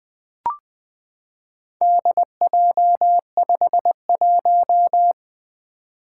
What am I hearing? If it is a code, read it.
Morse code, heard as a keyed tone, about 20 wpm, 700 Hz. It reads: DJ51
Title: Four Letter-Number Combinations - 20wpm